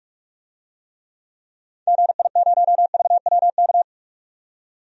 Morse code, heard as a keyed tone, about 46 wpm, 700 Hz. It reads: GI0VWK
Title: DXCC-100 Call Signs - 6 Characters - 45wpm